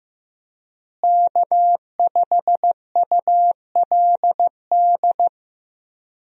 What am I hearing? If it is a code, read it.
Morse code, heard as a keyed tone, about 15 wpm, 700 Hz. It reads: K5ULD
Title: USA Call Signs - 5 Characters - Repeated 2x - 15wpm